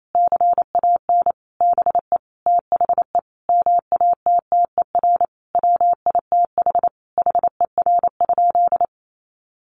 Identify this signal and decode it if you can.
Morse code, heard as a keyed tone, about 28 wpm, 700 Hz. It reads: CAN BE THE MATTER WITH HER?
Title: Sentences from Top 700 Words 28wpm